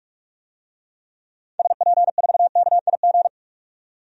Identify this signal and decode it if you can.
Morse code, heard as a keyed tone, about 41 wpm, 700 Hz. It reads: SP4KIG
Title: DXCC-100 Call Signs - 6 Characters - 45wpm